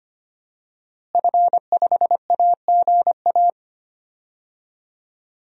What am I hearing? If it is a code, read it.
Morse code, heard as a keyed tone, about 25 wpm, 700 Hz. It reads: F5AGA
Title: DXCC-100 Call Signs - 5 Characters - Repeated 2x - 25wpm